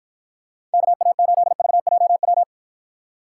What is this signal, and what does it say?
Morse code, heard as a keyed tone, about 52 wpm, 700 Hz. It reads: KA8VJW